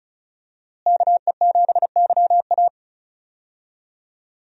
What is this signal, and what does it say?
Morse code, heard as a keyed tone, about 36 wpm, 700 Hz. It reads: KE7YA
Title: USA Call Signs - 5 Characters - Repeated 3x - 35wpm